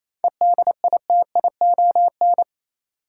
Morse code, heard as a keyed tone, about 27 wpm, 700 Hz. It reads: EDITION
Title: Sets of 5 Words - New Word Drill - 28wpm